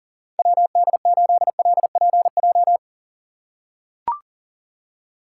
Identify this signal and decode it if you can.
Morse code, heard as a keyed tone, about 40 wpm, 700 Hz. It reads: WD8LPJ
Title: USA Call Signs - 6 Characters - 6x Character Spacing - 40wpm